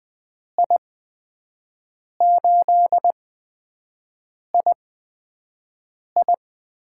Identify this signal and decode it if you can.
Morse code, heard as a keyed tone, about 20 wpm, spaced as for 5 wpm, 700 Hz. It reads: I8II
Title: USA Call Signs - 8x Character Spacing - 20wpm